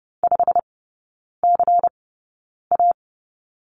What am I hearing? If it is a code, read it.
Morse code, heard as a keyed tone, about 30 wpm, spaced as for 9 wpm, 700 Hz. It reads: 5CA
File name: Lesson 19e - Letter w - Three Characters 30wpm